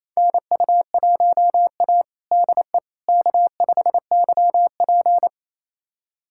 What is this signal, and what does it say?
Morse code, heard as a keyed tone, about 28 wpm, 700 Hz. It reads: NU1A DE K5YP
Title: USA Callsign DE Callsign - 4 Characters - 28wpm